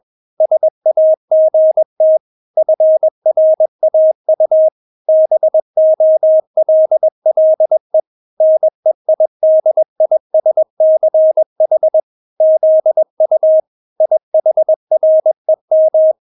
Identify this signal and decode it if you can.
Morse code, heard as a keyed tone, about 21 wpm, 630 Hz. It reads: SAGT FRAU BOLLE NEIDISCH ZU IHREM
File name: Witz2 21wpm